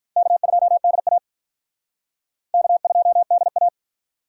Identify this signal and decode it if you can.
Morse code, heard as a keyed tone, about 45 wpm, 700 Hz. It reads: K2DA K2DA